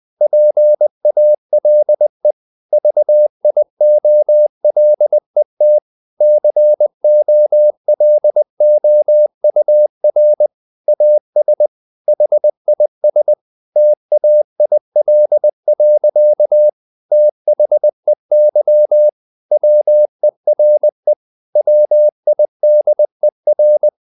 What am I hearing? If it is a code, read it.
Morse code, heard as a keyed tone, about 20 wpm, 600 Hz. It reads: PALE VIOLET COLOUR AS HIS TAIL. THEY WERE WIDER
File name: the old man and the sea part 2 20wpm